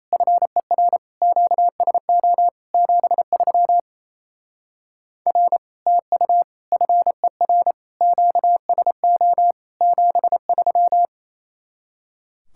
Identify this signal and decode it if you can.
Morse code, heard as a keyed tone, about 33 wpm, 700 Hz. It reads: FER QSO 73 R TU FER QSO 73
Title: Speed-Racing Common QSO Phrases Encore - 28wpm